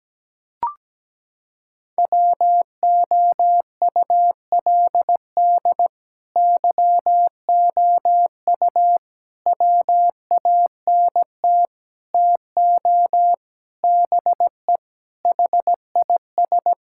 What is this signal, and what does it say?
Morse code, heard as a keyed tone, about 17 wpm, 700 Hz. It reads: WOULD YOU WANT TO BE HIS
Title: Sentences from Top 1600 Words 17wpm